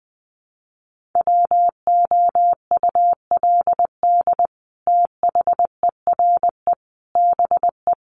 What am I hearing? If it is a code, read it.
Morse code, heard as a keyed tone, about 19 wpm, 700 Hz. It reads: WOULD THERE BE
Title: Sentences from Top 1500 Words 20wpm